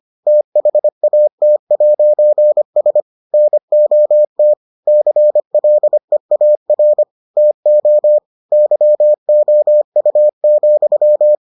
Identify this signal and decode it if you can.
Morse code, heard as a keyed tone, about 25 wpm, 600 Hz. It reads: THAT'S NOT CLEAR TO YOU,